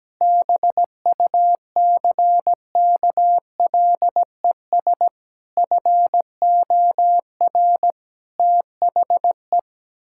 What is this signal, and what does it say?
Morse code, heard as a keyed tone, about 17 wpm, 700 Hz. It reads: BUCKLES FOR THE
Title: Pangram Challenge v2 - 17wpm